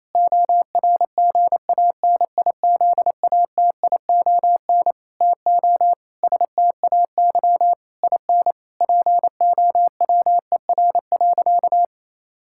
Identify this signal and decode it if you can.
Morse code, heard as a keyed tone, about 28 wpm, 700 Hz. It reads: ORGANIZATION TO STAY IN POWER.
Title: Sentences from Top 1000 Words - Review Part 5 - 28wpm